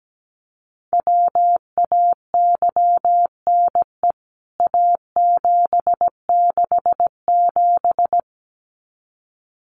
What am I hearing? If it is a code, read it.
Morse code, heard as a keyed tone, about 17 wpm, 700 Hz. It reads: WAYNE A767
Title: CWT Exchange 17wpm